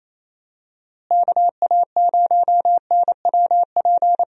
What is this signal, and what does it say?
Morse code, heard as a keyed tone, about 28 wpm, 700 Hz. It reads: KA0NWP
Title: USA Call Signs - 6 Characters - Repeated 2x - 28wpm